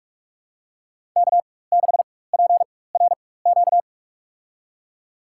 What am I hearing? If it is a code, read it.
Morse code, heard as a keyed tone, about 46 wpm, spaced as for 21 wpm, 700 Hz. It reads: K6PRQ
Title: USA Call Signs - 5 Characters - 4x Character Spacing - 45wpm